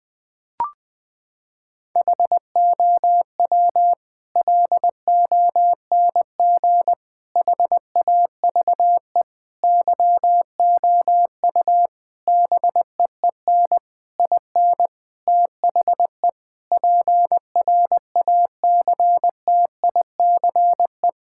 Morse code, heard as a keyed tone, about 20 wpm, 700 Hz. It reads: HOW LONG HAVE YOU BEEN IN THE PRACTICE
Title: Sentences from Top 1000 Words - Review Part 6 - 20wpm